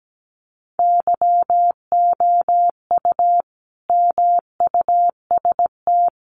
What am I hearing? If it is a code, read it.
Morse code, heard as a keyed tone, about 17 wpm, 700 Hz. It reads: YOU MUST
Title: Sentences from Top 1100 Words 17wpm